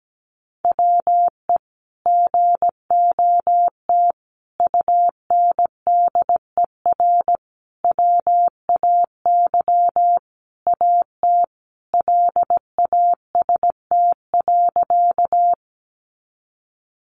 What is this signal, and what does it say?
Morse code, heard as a keyed tone, about 17 wpm, 700 Hz. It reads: WE GOT UNDER WAY AT LAST.
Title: Sentences from Top 300 Words 17wpm